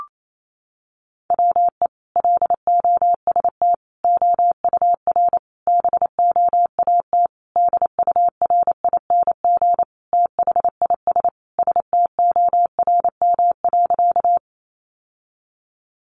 Morse code, heard as a keyed tone, about 28 wpm, 700 Hz. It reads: WE LOST OUR BOAT DURING THIS STORM.
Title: Sentences from Top 1500 Words 28wpm